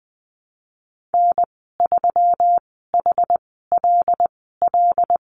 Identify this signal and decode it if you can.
Morse code, heard as a keyed tone, about 20 wpm, spaced as for 15 wpm, 700 Hz. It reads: N3HLL
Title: 128 Lesson 21g - Number 6 - Call Signs 20wpm